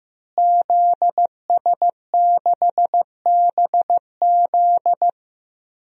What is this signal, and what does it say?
Morse code, heard as a keyed tone, about 15 wpm, 700 Hz. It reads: ZS6BZ
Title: DXCC-100 Call Signs - 5 Characters - Repeated 2x - 15wpm